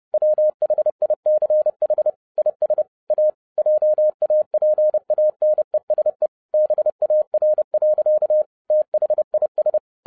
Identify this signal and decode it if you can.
Morse code, heard as a keyed tone, about 30 wpm, 610 Hz. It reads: WHICH IS A JAPANESE BAR. THIS
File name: 2011-07-mp3 / n60 BACK FROM A BIG ACCIDENT 30wpm